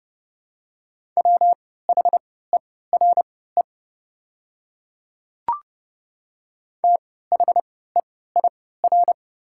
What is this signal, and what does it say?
Morse code, heard as a keyed tone, about 30 wpm, spaced as for 17 wpm, 700 Hz. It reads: WHERE THEIR
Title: 3x Character Spacing - Top 300 Words - 30wpm